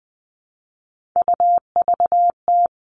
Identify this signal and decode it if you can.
Morse code, heard as a keyed tone, about 20 wpm, 700 Hz. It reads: UVT
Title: All Cut Numbers - 3-Digit 20wpm